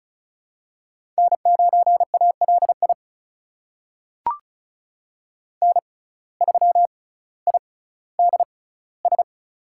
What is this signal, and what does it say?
Morse code, heard as a keyed tone, about 36 wpm, spaced as for 18 wpm, 700 Hz. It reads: N9ALI N3IDS
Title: USA Call Signs - 5 Characters - 6x Character Spacing - 35wpm